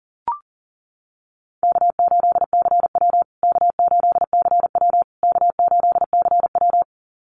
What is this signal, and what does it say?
Morse code, heard as a keyed tone, about 37 wpm, 700 Hz. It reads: K8CW K8CW K8CW
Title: USA Call Signs Repeated 3x 40wpm